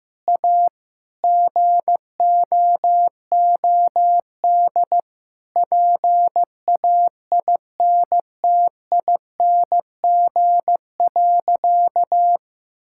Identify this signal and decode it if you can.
Morse code, heard as a keyed tone, about 15 wpm, 700 Hz. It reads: A GOOD PAINTING.
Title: Sentences from Top 1000 Words - Review Part 6 - 15wpm